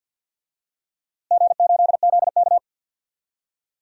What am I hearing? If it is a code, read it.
Morse code, heard as a keyed tone, about 47 wpm, 700 Hz. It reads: G8ZK
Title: DXCC-100 Call Signs - 4 Characters - Repeated 2x - 50wpm